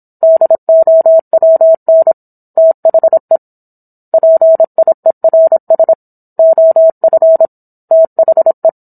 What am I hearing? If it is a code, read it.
Morse code, heard as a keyed tone, about 27 wpm, 650 Hz. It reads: DOWN THE PIERS OF THE
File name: Book One / 17-Chapter 17 - 26 WPM